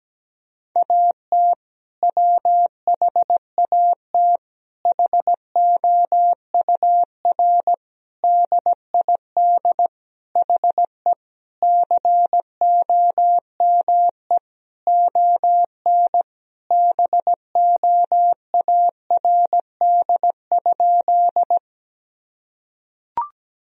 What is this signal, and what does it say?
Morse code, heard as a keyed tone, about 17 wpm, 700 Hz. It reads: AT WHAT HOUR DID HE COME ON BOARD?
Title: Sentences from Top 1000 Words - Review Part 4 - 17wpm